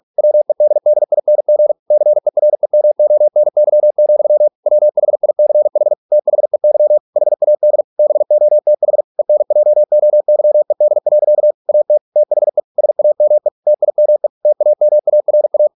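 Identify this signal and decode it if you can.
Morse code, heard as a keyed tone, about 46 wpm, 600 Hz. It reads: WEDDING CEREMONY, WHICH THEY HAD BOTH ENJOYED. AT THE SAME TIME TAMARA
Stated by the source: Morse practice at 45 wpm, the novel Never by Ken Follett in English, from CW med Gnister.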